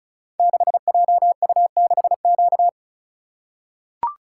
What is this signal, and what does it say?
Morse code, heard as a keyed tone, about 35 wpm, 700 Hz. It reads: 6JU6Q
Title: Code Groups - 5 Letter:Numbers - 35wpm